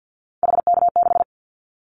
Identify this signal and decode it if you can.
Morse code, heard as a keyed tone, about 50 wpm, 700 Hz. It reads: HK6
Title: Callsign Prefixes for Top 100 DXCC - 50wpm